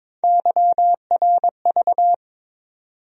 Morse code, heard as a keyed tone, about 22 wpm, 700 Hz. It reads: YRV
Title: USA Call Sign Postfixes - 3 Letters 22wpm